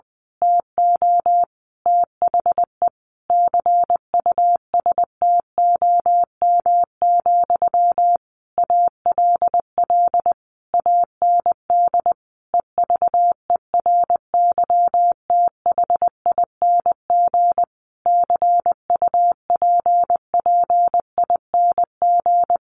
Morse code, heard as a keyed tone, about 20 wpm, 700 Hz. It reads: TO THE CUSTOM, ALL AND EVERYTHING CUPPING
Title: A Sound Of Thunder 20wpm